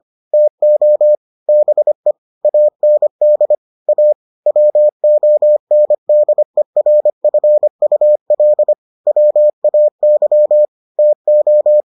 Morse code, heard as a keyed tone, about 25 wpm, 600 Hz. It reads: TO BE AND A WONDERFUL WAY TO